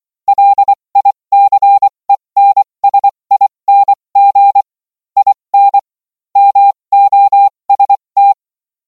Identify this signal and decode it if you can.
Morse code, heard as a keyed tone, about 24 wpm, spaced as for 20 wpm, 800 Hz. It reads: LICENSING IN MOST